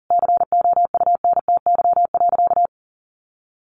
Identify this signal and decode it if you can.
Morse code, heard as a keyed tone, about 37 wpm, 700 Hz. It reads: COUNTY.
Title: Sentences from Top 1200 Words 40wpm